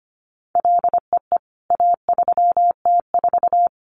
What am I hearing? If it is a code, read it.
Morse code, heard as a keyed tone, about 25 wpm, 700 Hz. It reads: LEE A3T4
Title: CWT Exchange 25wpm